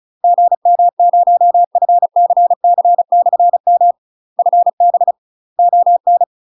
Morse code, heard as a keyed tone, about 35 wpm, 700 Hz. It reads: GM0FCC/M FB ON
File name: UK MASTER QSOs TRAINING / 35WPM / UK MASTER QSOs 0007